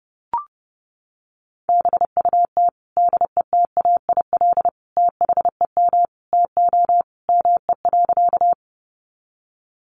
Morse code, heard as a keyed tone, about 30 wpm, 700 Hz. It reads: BUT DETAIL THEM TO ME.
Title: Sentences from Top 1500 Words 30wpm